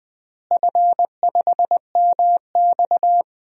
Morse code, heard as a keyed tone, about 20 wpm, 700 Hz. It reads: F5MX
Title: DXCC-100 Call Signs - 4 Characters - Repeated 3x - 20wpm